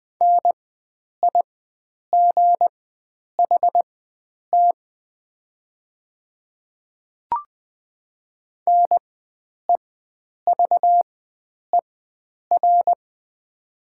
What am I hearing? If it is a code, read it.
Morse code, heard as a keyed tone, about 20 wpm, spaced as for 9 wpm, 700 Hz. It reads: NIGHT NEVER
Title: 4x Character Spacing - Top 300 Words - 20wpm